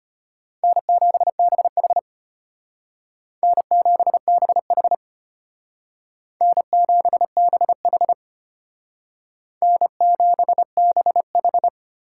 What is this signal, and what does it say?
Morse code, heard as a keyed tone, about 32 wpm, 700 Hz. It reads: N7BH N7BH N7BH N7BH